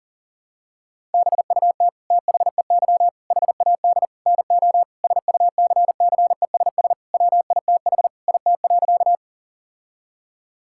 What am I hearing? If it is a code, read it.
Morse code, heard as a keyed tone, about 40 wpm, 700 Hz. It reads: BUT THEY HAD NO SUCCESS WITH IT.
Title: Sentences from Top 1000 Words - Review Part 6 - 40wpm